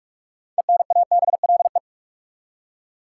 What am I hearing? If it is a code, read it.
Morse code, heard as a keyed tone, about 46 wpm, 700 Hz. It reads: ENABLE
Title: Six Letter Words - 45wpm